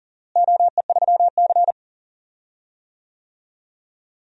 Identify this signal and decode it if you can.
Morse code, heard as a keyed tone, about 40 wpm, 700 Hz. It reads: OE3C
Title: DXCC-100 Call Signs - 4 Characters - Repeated 3x - 40wpm